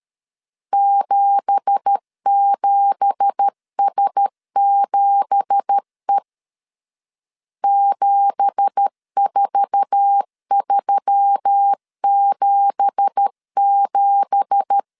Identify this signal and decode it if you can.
Morse code, heard as a keyed tone, about 13 wpm, 800 Hz. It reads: 77S7E 74377
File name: LEZIONI / 9 2867 12wpm 2322